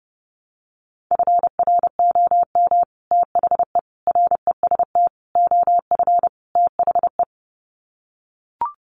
Morse code, heard as a keyed tone, about 28 wpm, 700 Hz. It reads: FROM THE REST OF THE
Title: Sets of 5 Words 30wpm